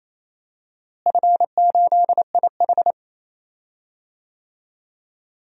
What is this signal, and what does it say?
Morse code, heard as a keyed tone, about 27 wpm, 700 Hz. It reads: F8IH
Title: DXCC-100 Call Signs - 4 Characters - Repeated 3x - 28wpm